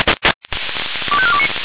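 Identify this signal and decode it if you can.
Five-tone selective call (ZVEI), heard as quick runs of five tones, tones of about 70 ms. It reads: 25429
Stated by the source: ZVEI five-tone selective call received on an SDR.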